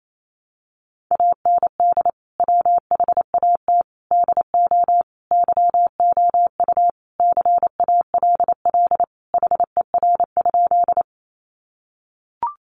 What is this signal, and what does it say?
Morse code, heard as a keyed tone, about 28 wpm, 700 Hz. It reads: AND WHAT DO YOU CALL HER?
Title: Sentences from Top 100 Words 28wpm